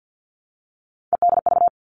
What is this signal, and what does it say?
Morse code, heard as a keyed tone, about 50 wpm, 700 Hz. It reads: EDV
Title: All Cut Numbers - 3-Digit 50wpm